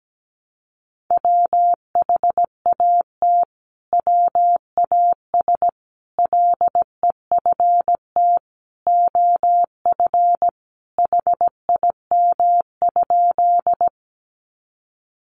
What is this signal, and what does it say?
Morse code, heard as a keyed tone, about 17 wpm, 700 Hz. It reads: WHAT WAS LEFT OF HIM?
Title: Sentences from Top 300 Words 17wpm